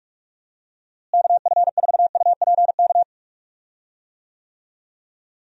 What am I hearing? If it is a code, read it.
Morse code, heard as a keyed tone, about 41 wpm, 700 Hz. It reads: KF4UPK
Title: USA Call Signs - 6 Characters - Repeated 3x - 45wpm